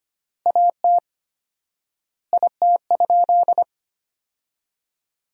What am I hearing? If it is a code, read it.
Morse code, heard as a keyed tone, about 25 wpm, 700 Hz. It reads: AT IT?
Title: 4x Word-Spacing - Sentences from Top 100 Words - 25wpm